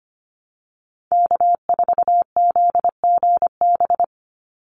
Morse code, heard as a keyed tone, about 25 wpm, 700 Hz. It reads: K4ZGB
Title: Call Signs Repeated 2x - 25wpm